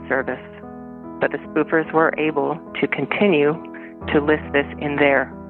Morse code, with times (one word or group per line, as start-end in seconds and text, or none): none